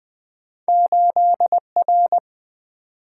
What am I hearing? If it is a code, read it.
Morse code, heard as a keyed tone, about 27 wpm, 700 Hz. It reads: 8R